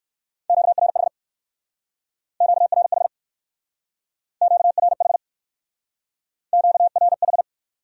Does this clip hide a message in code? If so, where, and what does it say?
Yes: Morse code from 0.5 to 7.4 s, heard as a keyed tone, about 64 wpm, 700 Hz. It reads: QRH QRH QRH QRH